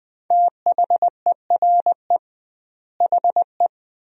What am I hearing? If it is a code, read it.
Morse code, heard as a keyed tone, about 20 wpm, 700 Hz. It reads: THERE HE
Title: 2x Word-Spacing - Sentences from Top 100 Words - 20wpm